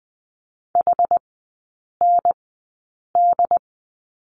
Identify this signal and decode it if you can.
Morse code, heard as a keyed tone, about 20 wpm, spaced as for 8 wpm, 700 Hz. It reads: HND
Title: 067 Lesson 12e - Letter d - Three Characters 20wpm